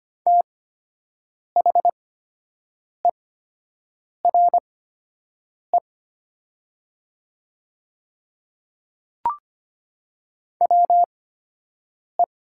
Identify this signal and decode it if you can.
Morse code, heard as a keyed tone, about 25 wpm, spaced as for 7 wpm, 700 Hz. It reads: THERE WE
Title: 8x Character Spacing - Top 100 Words - 25wpm